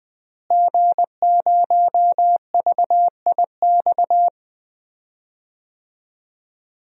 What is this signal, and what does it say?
Morse code, heard as a keyed tone, about 20 wpm, 700 Hz. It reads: G0VIX
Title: DXCC-100 Call Signs - 5 Characters - Repeated 3x - 20wpm